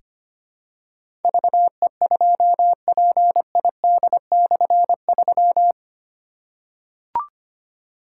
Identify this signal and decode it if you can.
Morse code, heard as a keyed tone, about 25 wpm, 700 Hz. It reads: VE2PID/3
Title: K1USN SST QSO Practice - 25wpm